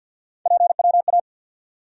Morse code, heard as a keyed tone, about 46 wpm, 700 Hz. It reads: PWA
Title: USA Call Sign Postfixes - 3 Letters 50wpm